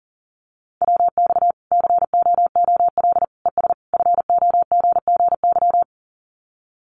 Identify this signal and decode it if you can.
Morse code, heard as a keyed tone, about 40 wpm, 700 Hz. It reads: WX COOL ES FOGGY
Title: Common QSO Phrases 40wpm